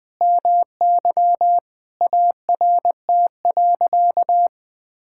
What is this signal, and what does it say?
Morse code, heard as a keyed tone, about 21 wpm, 700 Hz. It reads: MY ART.
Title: Sentences from Top 1000 Words - Review Part 6 - 20wpm